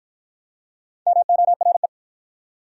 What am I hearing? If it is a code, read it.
Morse code, heard as a keyed tone, about 53 wpm, 700 Hz. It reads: MORE